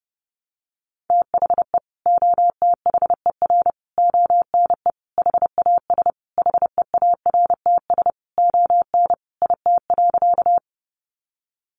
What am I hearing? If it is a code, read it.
Morse code, heard as a keyed tone, about 30 wpm, 700 Hz. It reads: THE OTHER ONE HAS HEARTS ON IT.
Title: Additional Easy Phrases 30wpm